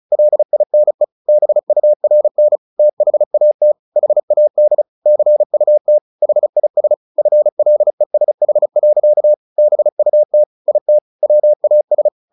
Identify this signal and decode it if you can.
Morse code, heard as a keyed tone, about 36 wpm, 600 Hz. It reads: LINE BURN THAT HAD CUT HIS FLESH. BUT IT WAS
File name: the old man and the sea / the old man and the sea 1 35wpm